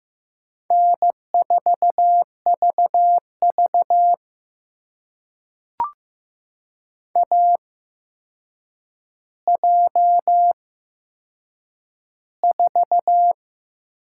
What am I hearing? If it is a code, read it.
Morse code, heard as a keyed tone, about 15 wpm, spaced as for 7 wpm, 700 Hz. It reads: N4VV AJ4
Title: USA Call Signs - 8x Character Spacing - 15wpm